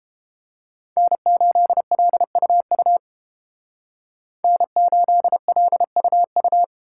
Morse code, heard as a keyed tone, about 30 wpm, 700 Hz. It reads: N8LUU N8LUU